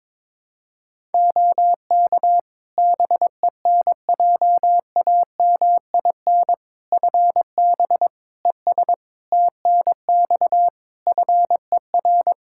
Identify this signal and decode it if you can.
Morse code, heard as a keyed tone, about 22 wpm, 700 Hz. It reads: OK BENJAMIN FB ES TNX FER
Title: K1USN SST QSO Practice 22wpm